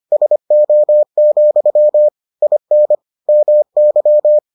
Morse code, heard as a keyed tone, about 25 wpm, 600 Hz. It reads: SO, IN MY